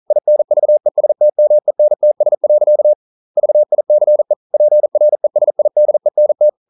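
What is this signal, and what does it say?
Morse code, heard as a keyed tone, about 40 wpm, 600 Hz. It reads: INVESTMENTS. VICE PRESIDENT
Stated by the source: Morse code practice at 41 wpm, from CW med Gnister.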